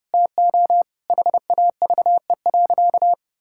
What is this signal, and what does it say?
Morse code, heard as a keyed tone, about 30 wpm, 700 Hz. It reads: TO HAVE.
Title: Sentences from Top 1000 Words - Review Part 5 - 30wpm